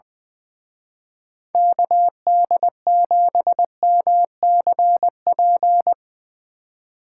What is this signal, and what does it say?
Morse code, heard as a keyed tone, about 22 wpm, 700 Hz. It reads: KD7MCP
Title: Speed Racing - USA 2x3 Call Signs 20wpm